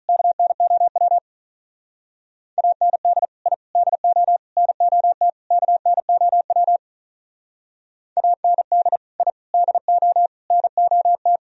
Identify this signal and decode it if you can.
Morse code, heard as a keyed tone, about 42 wpm, 700 Hz. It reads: KNOW AND I DO NOT KNOW AND I DO NOT
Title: Speed-Racing - Sets of 5 Words 35wpm